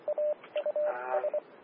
Morse code, heard as a keyed tone, about 24 wpm, spaced as for 20 wpm, 600 Hz, with radio noise. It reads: AUS
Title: Austin, TX Amateur Repeaters